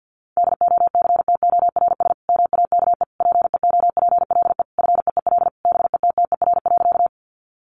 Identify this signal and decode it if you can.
Morse code, heard as a keyed tone, about 50 wpm, 700 Hz. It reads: DOCTORS MAKE PEOPLE FEEL BETTER.
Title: Easy Phrases 50wpm